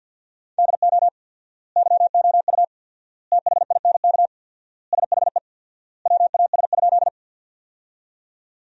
Morse code, heard as a keyed tone, about 50 wpm, 700 Hz. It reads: DO YOU THINK SHE WAS?
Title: 4x Word-Spacing - Sentences from Top 200 Words - 50wpm